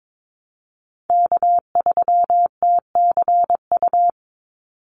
Morse code, heard as a keyed tone, about 22 wpm, 700 Hz. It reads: K3TCU
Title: NA SOTA Chaser Callsigns 22wpm